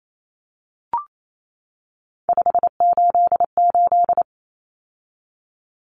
Morse code, heard as a keyed tone, about 28 wpm, 700 Hz. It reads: 588
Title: Numbers - 3-Digit 28wpm